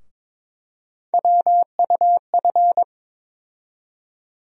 Morse code, heard as a keyed tone, about 22 wpm, 700 Hz. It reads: WUF
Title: USA Call Sign Postfixes - 3 Letters 22wpm